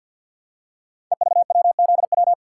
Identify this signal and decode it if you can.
Morse code, heard as a keyed tone, about 53 wpm, 700 Hz. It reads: EVWZW
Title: Code Groups - 5 Letter:Numbers - 50wpm